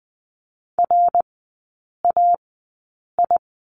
Morse code, heard as a keyed tone, about 20 wpm, spaced as for 8 wpm, 700 Hz. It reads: RAI
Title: 074 Lesson 13e - Letter L - Three Characters 20wpm